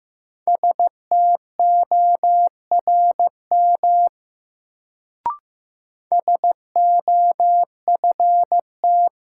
Morse code, heard as a keyed tone, about 15 wpm, 700 Hz. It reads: STORM SOFT
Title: Sentences from Top 1500 Words - New Word Drill - 15wpm